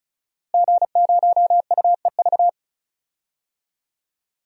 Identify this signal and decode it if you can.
Morse code, heard as a keyed tone, about 34 wpm, 700 Hz. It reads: G0UEV
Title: DXCC-100 Call Signs - 5 Characters - Repeated 3x - 35wpm